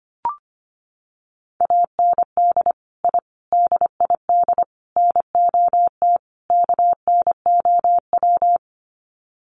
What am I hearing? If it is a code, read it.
Morse code, heard as a keyed tone, about 25 wpm, 690 Hz. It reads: AND I DID NOT KNOW
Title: Sets of 5 Words Encore 25wpm